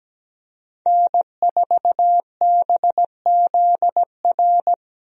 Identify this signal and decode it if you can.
Morse code, heard as a keyed tone, about 17 wpm, 700 Hz. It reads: N4BZR
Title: USA Call Signs - 5 Characters - 17wpm